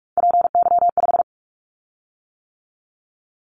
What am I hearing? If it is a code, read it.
Morse code, heard as a keyed tone, about 46 wpm, 700 Hz. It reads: PY5
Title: Callsign Prefixes for Top 100 DXCC - 45wpm